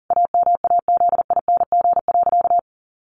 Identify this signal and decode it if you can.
Morse code, heard as a keyed tone, about 40 wpm, 700 Hz. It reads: AMAZING.
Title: Sentences from Top 1500 Words 40wpm